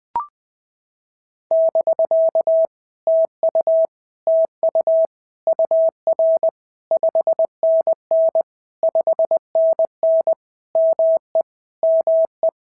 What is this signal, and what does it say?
Morse code, heard as a keyed tone, about 19 wpm, 650 Hz. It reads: <BK> TU TU UR 5NN 5NN ME ME
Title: POTA QSO Practice - 20wpm